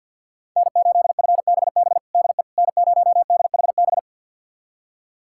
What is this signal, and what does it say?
Morse code, heard as a keyed tone, about 52 wpm, 700 Hz. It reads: N8FBB DE N0DHB
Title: USA Callsign DE Callsign - 5 Characters - 50wpm